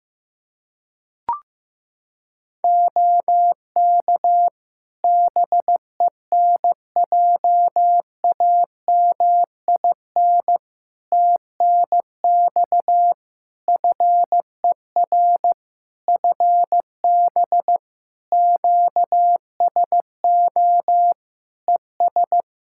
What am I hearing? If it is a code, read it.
Morse code, heard as a keyed tone, about 15 wpm, 700 Hz. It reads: OK BENJAMIN TNX FER FB QSO ES
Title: K1USN SST QSO Practice 15wpm